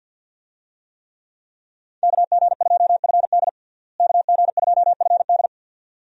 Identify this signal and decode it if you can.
Morse code, heard as a keyed tone, about 52 wpm, 700 Hz. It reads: KG2FD KG2FD
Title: USA Call Signs - 5 Characters - Repeated 2x - 50wpm